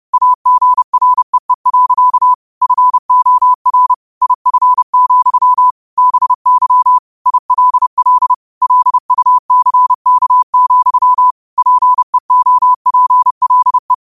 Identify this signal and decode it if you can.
Morse code, heard as a keyed tone, about 30 wpm, 1000 Hz. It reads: AGREE. FOR IF, BY ILL LUCK, PEOPLE
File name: qotdfiles / 30wpm